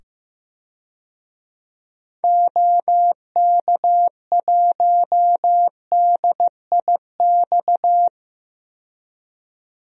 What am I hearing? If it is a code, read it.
Morse code, heard as a keyed tone, about 15 wpm, 700 Hz. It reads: OK1DIX
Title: DXCC-100 Call Signs - 6 Characters - 15wpm